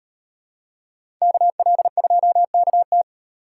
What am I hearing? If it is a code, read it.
Morse code, heard as a keyed tone, about 38 wpm, 700 Hz. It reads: KL2KT